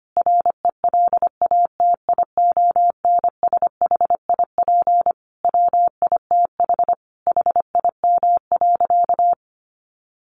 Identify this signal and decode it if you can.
Morse code, heard as a keyed tone, about 25 wpm, 700 Hz. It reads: RELATIONSHIP WITH HIM.
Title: Sentences from Top 1100 Words 25wpm